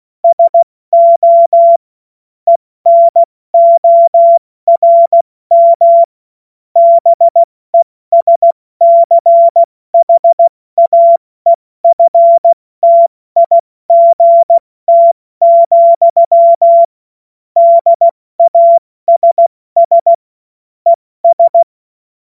Morse code, heard as a keyed tone, about 16 wpm, 680 Hz. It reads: SO ENORM BESCHAEFTIGT, DASS ES
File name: Morsekurs / Witze / Witz 10. Mai 2022 70 F80